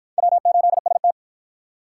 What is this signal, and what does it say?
Morse code, heard as a keyed tone, about 53 wpm, 700 Hz. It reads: W8ST